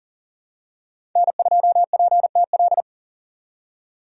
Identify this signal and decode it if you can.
Morse code, heard as a keyed tone, about 40 wpm, 700 Hz. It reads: N2PTL
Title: USA Call Signs - 5 Characters - 40wpm